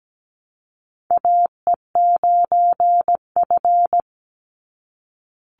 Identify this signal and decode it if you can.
Morse code, heard as a keyed tone, about 17 wpm, 700 Hz. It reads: AE9F
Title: NA SOTA Chaser Callsigns 17wpm